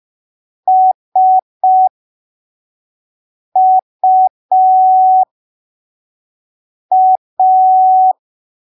Morse code, heard as a keyed tone, about 5 wpm, spaced as for 3 wpm, 750 Hz. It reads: SUA